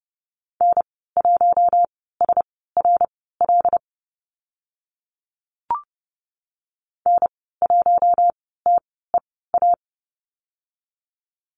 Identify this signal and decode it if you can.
Morse code, heard as a keyed tone, about 30 wpm, spaced as for 17 wpm, 700 Hz. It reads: N1SRL N1TEA
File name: Lesson 14g - Number 2 - Call Signs 30wpm